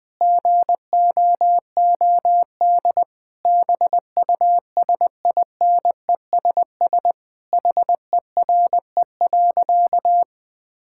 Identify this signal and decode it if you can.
Morse code, heard as a keyed tone, about 20 wpm, 700 Hz. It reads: GOOD BUSINESS HERE.
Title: Sentences from Top 1000 Words - Review Part 5 - 20wpm